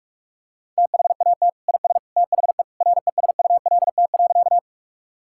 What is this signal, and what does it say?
Morse code, heard as a keyed tone, about 45 wpm, 700 Hz. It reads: THAT IS THE RESULT.
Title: Sentences from Top 1000 Words - Review Part 5 - 45wpm